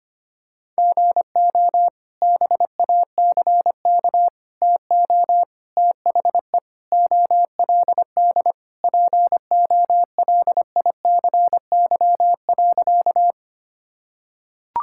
Morse code, heard as a keyed tone, about 25 wpm, 700 Hz. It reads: GO BACK TO THE OLD POLICY.
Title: Sentences from Top 1000 Words - Review Part 5 - 25wpm